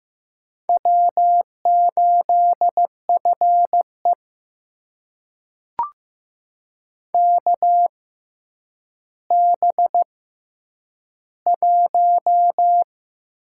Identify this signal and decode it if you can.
Morse code, heard as a keyed tone, about 15 wpm, spaced as for 7 wpm, 700 Hz. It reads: W8FEK B1
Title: USA Call Signs - 6x Character Spacing - 15wpm